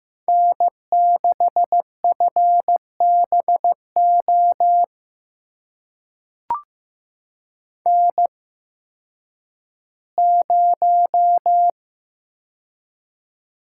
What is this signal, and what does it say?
Morse code, heard as a keyed tone, about 15 wpm, spaced as for 10 wpm, 700 Hz. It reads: N6FBO N0
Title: USA Call Signs - 5 Characters - 8x Character Spacing - 15wpm